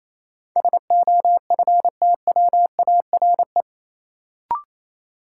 Morse code, heard as a keyed tone, about 28 wpm, 700 Hz. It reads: SOFTWARE
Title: Sentences from Top 1000 Words - Review Part 3 - New Word Drill - 28wpm